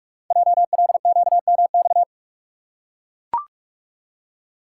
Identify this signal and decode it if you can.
Morse code, heard as a keyed tone, about 44 wpm, 690 Hz. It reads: JLQMX
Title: Code Groups - 5 Letters - 45wpm